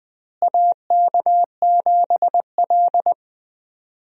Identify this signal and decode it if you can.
Morse code, heard as a keyed tone, about 21 wpm, 700 Hz. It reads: AK7L
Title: USA Call Signs - 4 Characters - Repeated 3x - 20wpm